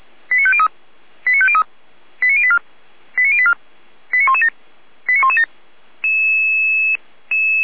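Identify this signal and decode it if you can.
Five-tone selective call (ZVEI), heard as quick runs of five tones, tones of about 70 ms. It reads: 89503 89503 89084 89084 89117 89117
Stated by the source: ZVEI five-tone call 89115 received on an SDR.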